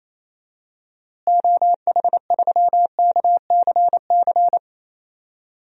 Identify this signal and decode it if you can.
Morse code, heard as a keyed tone, about 28 wpm, 700 Hz. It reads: OH3KCC
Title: DXCC-100 Call Signs - 6 Characters - Repeated 3x - 28wpm